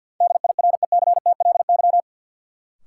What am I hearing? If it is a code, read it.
Morse code, heard as a keyed tone, about 53 wpm, 700 Hz. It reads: DIRECTLY
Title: Sentences from Top 800 Words - New Word Drill - 50wpm